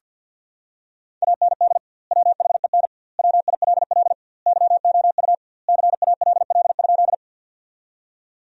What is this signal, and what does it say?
Morse code, heard as a keyed tone, about 50 wpm, 700 Hz. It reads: AND WHEN WILL YOU CALL?
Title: 2x Word-Spacing - Sentences from Top 100 Words - 50wpm